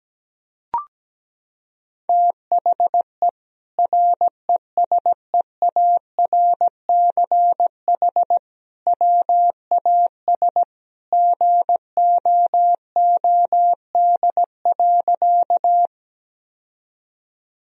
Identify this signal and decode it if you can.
Morse code, heard as a keyed tone, about 17 wpm, 700 Hz. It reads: THE RESEARCH WAS GOOD.
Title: Sentences from Top 1000 Words - Review Part 4 - 17wpm